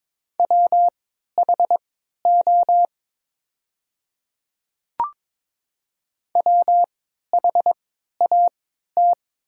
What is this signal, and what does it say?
Morse code, heard as a keyed tone, about 22 wpm, spaced as for 12 wpm, 700 Hz. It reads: WHO WHAT
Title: 3x Character Spacing - Top 100 Words - 22wpm